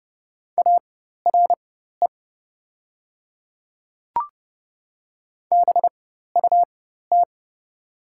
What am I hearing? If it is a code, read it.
Morse code, heard as a keyed tone, about 30 wpm, spaced as for 14 wpm, 700 Hz. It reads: ARE BUT
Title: 4x Character Spacing - Top 100 Words - 30wpm